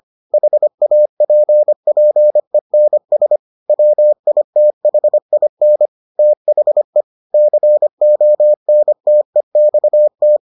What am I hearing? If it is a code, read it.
Morse code, heard as a keyed tone, about 25 wpm, 600 Hz. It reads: HAPPENS WITHIN THE CONTEXT